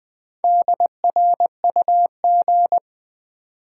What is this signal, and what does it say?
Morse code, heard as a keyed tone, about 20 wpm, 700 Hz. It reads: DRUG
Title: Sets of 5 Words - New Word Drill - 20wpm